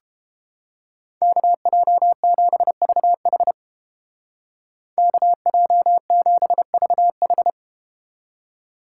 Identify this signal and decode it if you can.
Morse code, heard as a keyed tone, about 32 wpm, 700 Hz. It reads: KJ7VH KJ7VH